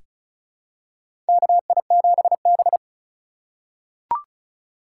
Morse code, heard as a keyed tone, about 36 wpm, 700 Hz. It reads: KI7B
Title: USA Call Signs - 4 Characters - 2x Character Spacing - 35wpm